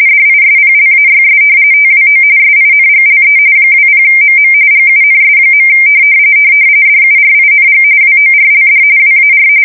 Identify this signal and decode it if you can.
RTTY, about 45.45 baud. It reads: DISPLAYS TWO SINGLE 20MM GUNS. DURING WORLD WAR
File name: fsk170 / battleshipnj